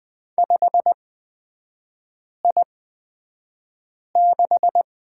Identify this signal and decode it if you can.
Morse code, heard as a keyed tone, about 20 wpm, spaced as for 5 wpm, 700 Hz. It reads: 5I6
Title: 224 Lesson 36e - BT - Three Characters 20wpm